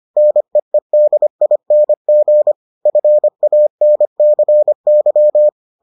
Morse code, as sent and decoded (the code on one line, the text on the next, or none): -. . -.. .. -. --. / ..-. .- -. -.-. -.--
NEEDING FANCY